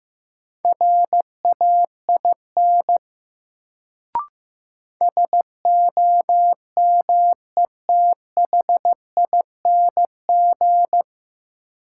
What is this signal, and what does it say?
Morse code, heard as a keyed tone, about 15 wpm, 700 Hz. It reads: RAIN SOMETHING
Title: Sentences from Top 600 Words - New Word Drill - 15wpm